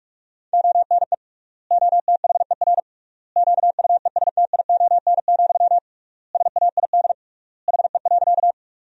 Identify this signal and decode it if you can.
Morse code, heard as a keyed tone, about 46 wpm, 700 Hz. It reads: ONE OTHER QUESTION, SAID HE.
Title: 3x Word-Spacing - Sentences from Top 400 Words - 45wpm